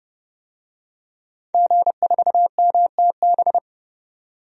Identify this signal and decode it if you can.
Morse code, heard as a keyed tone, about 30 wpm, 700 Hz. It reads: G4MTB
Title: DXCC-100 Call Signs - 5 Characters - 30wpm